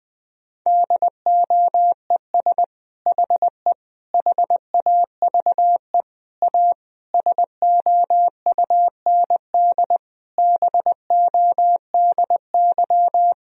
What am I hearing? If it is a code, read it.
Morse code, heard as a keyed tone, about 20 wpm, 700 Hz. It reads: DOES HE HAVE A SOUND BODY
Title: Sentences from Top 1000 Words - Review Part 2 - 20wpm